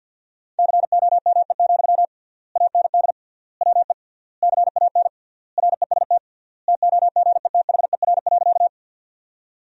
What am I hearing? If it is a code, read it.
Morse code, heard as a keyed tone, about 50 wpm, 700 Hz. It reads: COME, AND WE CAN REST TOGETHER.
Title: 3x Word-Spacing - Sentences from Top 500 Words - 50wpm